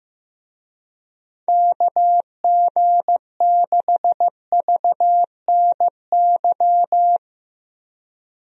Morse code, heard as a keyed tone, about 16 wpm, 700 Hz. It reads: KG6VNY
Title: USA Call Signs - 6 Characters - Repeated 3x - 15wpm